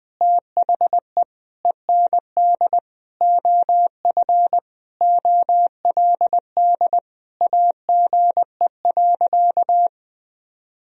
Morse code, heard as a keyed tone, about 20 wpm, 700 Hz. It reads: THE END OF OLD AGE.
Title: Sentences from Top 1000 Words - Review Part 3 - 20wpm